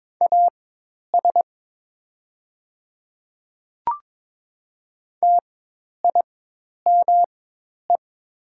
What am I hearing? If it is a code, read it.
Morse code, heard as a keyed tone, about 22 wpm, spaced as for 10 wpm, 700 Hz. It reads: AS TIME
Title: 4x Character Spacing - Top 200 Words - 22wpm